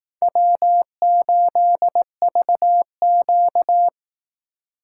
Morse code, heard as a keyed tone, about 18 wpm, 700 Hz. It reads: W8VQ
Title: Speed Racing - USA 1x2 Call Signs 15wpm